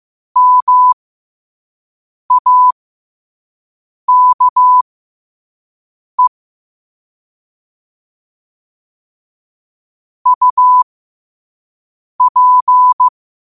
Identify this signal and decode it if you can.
Morse code, heard as a keyed tone, about 15 wpm, spaced as for 5 wpm, 1000 Hz. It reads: MAKE UP
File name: qotdfiles / 5wpm